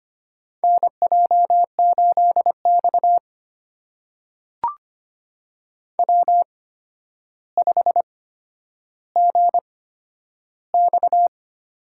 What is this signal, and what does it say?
Morse code, heard as a keyed tone, about 25 wpm, spaced as for 11 wpm, 700 Hz. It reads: NJ8X W5GX
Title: USA Call Signs - 4 Characters - 8x Character Spacing - 25wpm